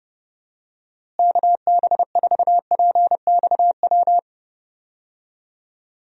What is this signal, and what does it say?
Morse code, heard as a keyed tone, about 30 wpm, 700 Hz. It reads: KB4PXW
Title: USA Call Signs - 6 Characters - Repeated 2x - 30wpm